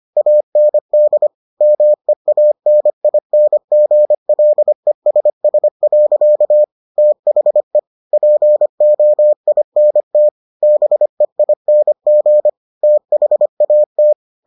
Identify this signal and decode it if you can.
Morse code, heard as a keyed tone, about 25 wpm, 600 Hz. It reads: AND MEANINGLESS. THE POINT BEING THAT